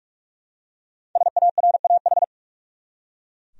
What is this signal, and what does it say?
Morse code, heard as a keyed tone, about 42 wpm, 700 Hz. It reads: SARAH
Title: USA Common Names - Max 5 Letters - 45wpm